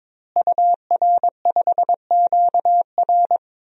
Morse code, heard as a keyed tone, about 22 wpm, 700 Hz. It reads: UR5QR
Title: DXCC-100 Call Signs - 5 Characters - Repeated 3x - 22wpm